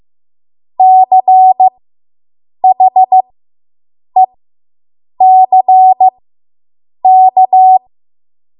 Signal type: Morse code